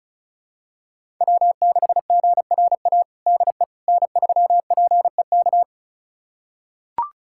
Morse code, heard as a keyed tone, about 36 wpm, 700 Hz. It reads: W6GRA DE N3PEK
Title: USA Callsign DE Callsign 35wpm